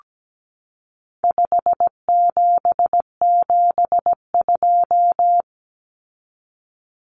Morse code, heard as a keyed tone, about 17 wpm, 700 Hz. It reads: 5772
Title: Common Cut Numbers - 4-Digit 17wpm